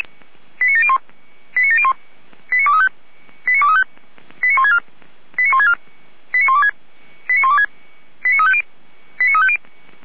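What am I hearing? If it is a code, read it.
Five-tone selective call (ZVEI), heard as quick runs of five tones, tones of about 70 ms. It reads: 89701 89701 89235 89235 89164 89164 89126 89126 89350 89350
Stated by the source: Five-tone ZVEI selective call 89590 received by SDR, channel 455.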